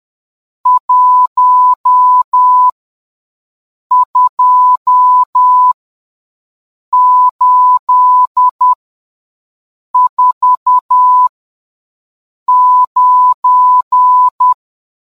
Morse code, heard as a keyed tone, about 10 wpm, spaced as for 5 wpm, 1000 Hz. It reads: 12849